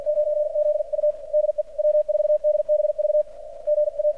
Morse code, heard as a keyed tone, about 60 wpm, 600 Hz, with radio noise. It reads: 9Z U DE W4DBV G A
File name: archived / CRSnet